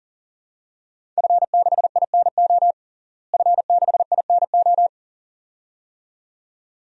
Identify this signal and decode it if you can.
Morse code, heard as a keyed tone, about 37 wpm, 700 Hz. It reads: F6INO F6INO
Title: DXCC-100 Call Signs - 5 Characters - Repeated 2x - 40wpm